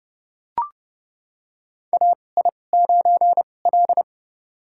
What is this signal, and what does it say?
Morse code, heard as a keyed tone, about 30 wpm, spaced as for 22 wpm, 700 Hz. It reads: AI9L
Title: USA Call Signs - 2x Character Spacing - 30wpm